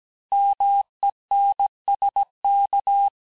Morse code, heard as a keyed tone, about 17 wpm, 800 Hz. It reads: MENSK
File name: corrige-cours-1581-texte-chiffre-vitesse-16-mpm